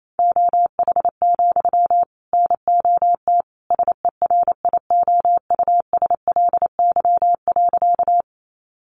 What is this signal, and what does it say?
Morse code, heard as a keyed tone, about 28 wpm, 700 Hz. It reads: OH, NOT SERIOUSLY.
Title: Sentences from Top 1200 Words 28wpm